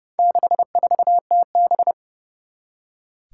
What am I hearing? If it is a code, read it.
Morse code, heard as a keyed tone, about 30 wpm, 700 Hz. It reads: 64TB
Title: Four Challenging Letter-Number Combinations 30wpm